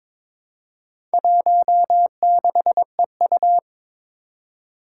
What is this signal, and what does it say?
Morse code, heard as a keyed tone, about 22 wpm, 700 Hz. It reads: 16EU
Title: Four Challenging Letter-Number Combinations 22wpm